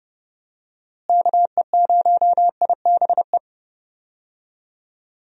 Morse code, heard as a keyed tone, about 30 wpm, 700 Hz. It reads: KE0IBE
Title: USA Call Signs - 6 Characters - Repeated 3x - 30wpm